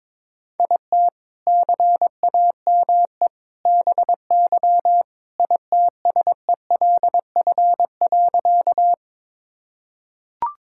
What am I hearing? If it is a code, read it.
Morse code, heard as a keyed tone, about 22 wpm, 700 Hz. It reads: IT CAME BY ITSELF.
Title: Sentences from Top 1000 Words - Review Part 5 - 22wpm